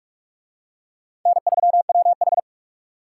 Morse code, heard as a keyed tone, about 43 wpm, 700 Hz. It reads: N3WH